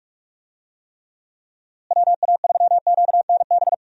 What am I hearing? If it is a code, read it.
Morse code, heard as a keyed tone, about 42 wpm, 700 Hz. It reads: WA3QNB
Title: USA Call Signs - 6 Characters - Repeated 2x - 45wpm